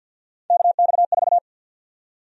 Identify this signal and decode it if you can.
Morse code, heard as a keyed tone, about 53 wpm, 700 Hz. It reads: KX4
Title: USA Call Sign Prefixes - 50wpm